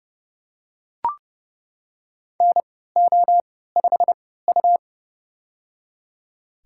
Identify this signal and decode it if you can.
Morse code, heard as a keyed tone, about 32 wpm, spaced as for 17 wpm, 700 Hz. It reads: NO5U
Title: USA Call Signs - 4 Characters - 3x Character Spacing - 30wpm